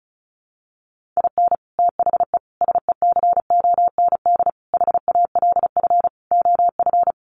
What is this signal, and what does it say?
Morse code, heard as a keyed tone, about 34 wpm, 700 Hz. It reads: IN THE SECOND HALF OF
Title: Sets of 5 Words 35wpm